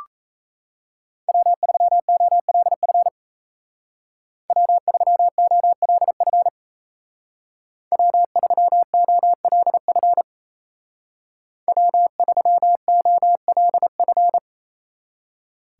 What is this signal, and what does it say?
Morse code, heard as a keyed tone, about 37 wpm, 700 Hz. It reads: W3OLF W3OLF W3OLF W3OLF